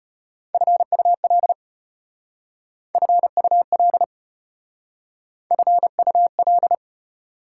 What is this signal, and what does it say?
Morse code, heard as a keyed tone, about 34 wpm, 690 Hz. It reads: FUL FUL FUL